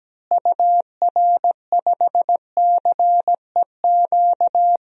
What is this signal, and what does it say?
Morse code, heard as a keyed tone, about 17 wpm, 700 Hz. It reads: UR5CEQ
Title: DXCC-100 Call Signs - 6 Characters - Repeated 2x - 17wpm